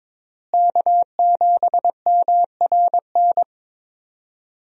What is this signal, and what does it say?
Morse code, heard as a keyed tone, about 22 wpm, 700 Hz. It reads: K7MRN
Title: USA Call Signs - 5 Characters - Repeated 3x - 22wpm